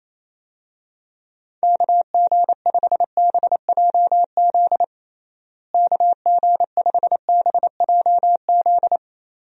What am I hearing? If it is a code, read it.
Morse code, heard as a keyed tone, about 28 wpm, 700 Hz. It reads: KG5BJZ KG5BJZ
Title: USA Call Signs - 6 Characters - Repeated 2x - 28wpm